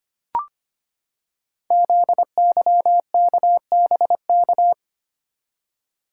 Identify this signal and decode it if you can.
Morse code, heard as a keyed tone, about 25 wpm, 700 Hz. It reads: ZYKBK
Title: Code Groups - 5 Letters - 25wpm